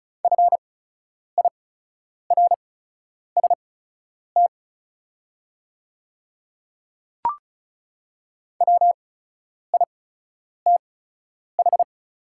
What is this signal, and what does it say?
Morse code, heard as a keyed tone, about 36 wpm, spaced as for 10 wpm, 700 Hz. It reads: FIRST WITH
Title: 8x Character Spacing - Top 100 Words - 35wpm